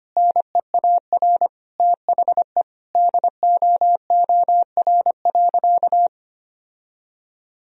Morse code, heard as a keyed tone, about 25 wpm, 700 Hz. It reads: NEAR THE DOOR.
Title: Sentences from Top 1000 Words - Review Part 2 - 25wpm